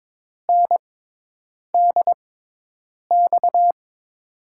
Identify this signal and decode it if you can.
Morse code, heard as a keyed tone, about 22 wpm, spaced as for 8 wpm, 700 Hz. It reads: NDX
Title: USA Call Signs - 6 Characters - 6x Character Spacing - 22wpm